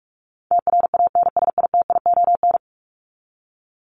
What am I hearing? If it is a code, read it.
Morse code, heard as a keyed tone, about 44 wpm, 700 Hz. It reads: TRANSITION
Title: Top 1751-2000 Words 45wpm